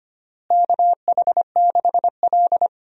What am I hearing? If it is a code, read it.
Morse code, heard as a keyed tone, about 25 wpm, 700 Hz. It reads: KH6L
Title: DXCC-100 Call Signs - 4 Characters - Repeated 3x - 25wpm